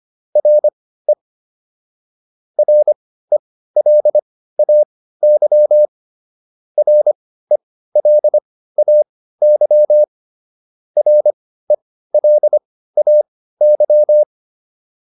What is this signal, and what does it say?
Morse code, heard as a keyed tone, about 25 wpm, spaced as for 15 wpm, 600 Hz. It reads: RE RELAY RELAY RELAY